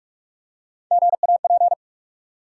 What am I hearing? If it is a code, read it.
Morse code, heard as a keyed tone, about 44 wpm, 700 Hz. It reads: GAP
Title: Three Letter Words - 45wpm